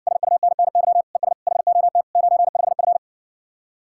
Morse code, heard as a keyed tone, about 60 wpm, 700 Hz. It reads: SUNNY ES HOT 95F